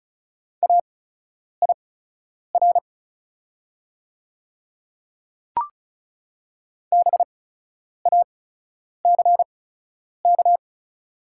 Morse code, heard as a keyed tone, about 36 wpm, spaced as for 10 wpm, 690 Hz. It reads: AIR BACK
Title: 8x Character Spacing - Top 500 Words - 35wpm